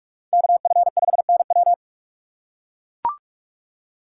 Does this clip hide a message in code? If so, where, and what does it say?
Yes: Morse code from 0.3 to 1.7 s, heard as a keyed tone, about 44 wpm, 700 Hz. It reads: KF5NW